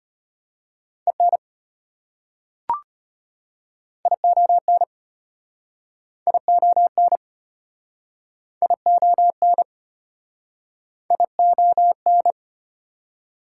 Morse code, heard as a keyed tone, about 34 wpm, 700 Hz. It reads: EN ION ION ION ION